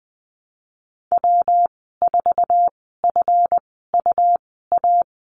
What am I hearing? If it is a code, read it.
Morse code, heard as a keyed tone, about 20 wpm, spaced as for 15 wpm, 700 Hz. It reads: W4FUA
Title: 147 Lesson 24g - Letter y - Call Signs 20wpm